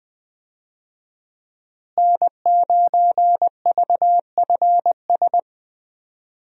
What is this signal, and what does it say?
Morse code, heard as a keyed tone, about 20 wpm, 700 Hz. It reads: N9VFS
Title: USA Call Signs - 5 Characters - Repeated 2x - 20wpm